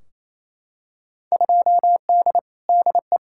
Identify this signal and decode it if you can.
Morse code, heard as a keyed tone, about 28 wpm, 700 Hz. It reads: 2D DE
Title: Field Day Exchanges Encore - 28wpm